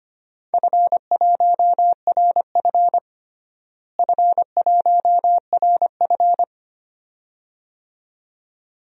Morse code, heard as a keyed tone, about 25 wpm, 700 Hz. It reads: F1RF F1RF
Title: DXCC-100 Call Signs - 4 Characters - Repeated 2x - 25wpm